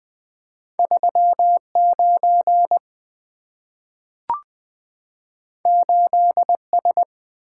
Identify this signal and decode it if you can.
Morse code, heard as a keyed tone, about 20 wpm, 700 Hz. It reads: 39 8S
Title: Two Letter-Number Combinations - 20wpm